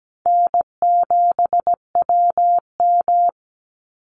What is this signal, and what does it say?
Morse code, heard as a keyed tone, about 17 wpm, 700 Hz. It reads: N7WM
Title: NA SOTA Chaser Callsigns 17wpm